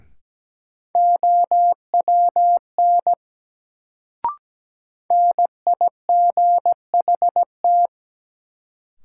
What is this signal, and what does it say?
Morse code, heard as a keyed tone, about 17 wpm, 700 Hz. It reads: OWN NIGHT
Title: Sentences from Top 300 Words - New Word Drill - 17wpm